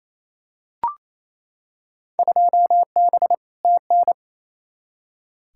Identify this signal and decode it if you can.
Morse code, heard as a keyed tone, about 28 wpm, 700 Hz. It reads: 2B TN
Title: Field Day Exchanges Encore - 28wpm